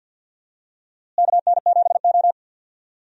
Morse code, heard as a keyed tone, about 52 wpm, 700 Hz. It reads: KN7O
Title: USA Call Signs - 4 Characters - Repeated 2x - 50wpm